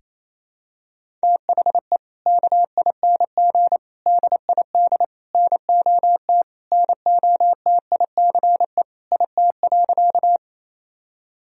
Morse code, heard as a keyed tone, about 28 wpm, 700 Hz. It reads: THE KING DID NOT NOTICE IT.
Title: Sentences from Top 1000 Words - Review Part 2 - 28wpm